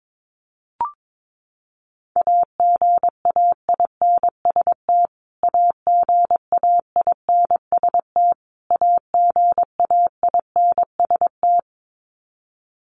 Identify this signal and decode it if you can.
Morse code, heard as a keyed tone, about 22 wpm, 700 Hz. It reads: AGAINST AGAINST AGAINST
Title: Top 500 Words Repeated 22wpm